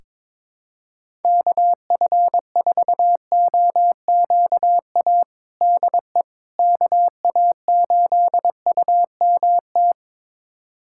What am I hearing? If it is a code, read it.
Morse code, heard as a keyed tone, about 22 wpm, 700 Hz. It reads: KF4OQA DE KA8UMT
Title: USA Callsign DE Callsign - 6 Characters - 22wpm